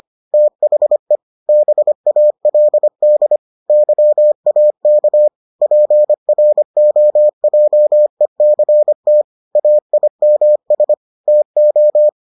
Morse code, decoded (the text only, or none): THE BALD YAK PROJECT AIMS TO